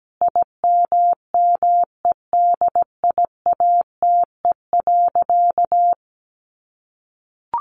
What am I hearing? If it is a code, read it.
Morse code, heard as a keyed tone, about 17 wpm, 700 Hz. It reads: IMMEDIATE.
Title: Sentences from Top 1200 Words 17wpm